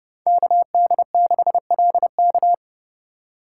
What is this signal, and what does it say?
Morse code, heard as a keyed tone, about 30 wpm, 700 Hz. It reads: KD6LK
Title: USA Call Signs - 5 Characters - 3x Character Spacing - 30wpm